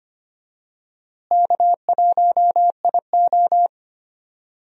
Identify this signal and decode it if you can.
Morse code, heard as a keyed tone, about 25 wpm, 700 Hz. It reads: K1IO
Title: USA Call Signs - 4 Characters - Repeated 2x - 25wpm